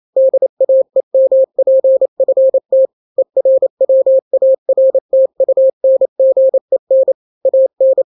Morse code, heard as a keyed tone, about 28 wpm, 530 Hz. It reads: DAEMPFT ERWARTUNGEN AN
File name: daily / 27wpm